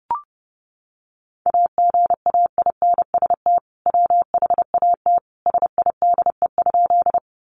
Morse code, heard as a keyed tone, about 30 wpm, 700 Hz. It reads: AGAINST WHAT SIDE?
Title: Sentences from Top 400 Words 30wpm